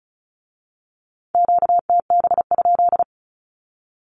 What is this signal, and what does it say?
Morse code, heard as a keyed tone, about 36 wpm, 700 Hz. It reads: QTB?
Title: Q-Codes with Spoken Meaning - 35wpm